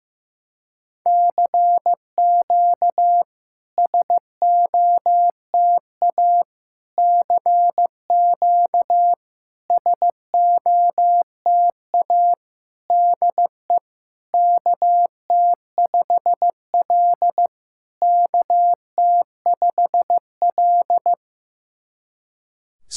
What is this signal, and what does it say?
Morse code, heard as a keyed tone, about 15 wpm, 700 Hz. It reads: CQ SOTA CQ SOTA DE KT5L KT5L
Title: SOTA QSO Practice 15wpm